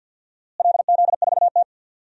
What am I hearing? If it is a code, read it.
Morse code, heard as a keyed tone, about 52 wpm, 700 Hz. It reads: LZ4T
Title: DXCC-100 Call Signs - 4 Characters - Repeated 2x - 50wpm